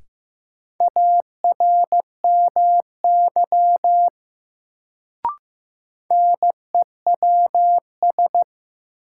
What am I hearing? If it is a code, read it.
Morse code, heard as a keyed tone, about 15 wpm, 700 Hz. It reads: ARMY NEWS
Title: Sentences from Top 800 Words - New Word Drill - 15wpm